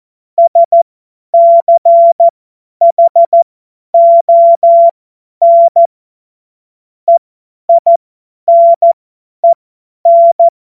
Morse code, heard as a keyed tone, about 14 wpm, spaced as for 10 wpm, 680 Hz. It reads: SCHON EINEN
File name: Morsekurs / Witze / Witz 17 Juni 2025 50 F70